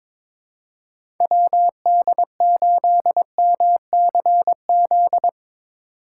Morse code, heard as a keyed tone, about 21 wpm, 700 Hz. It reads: WD8MCZ
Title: USA Call Signs - 6 Characters - 3x Character Spacing - 22wpm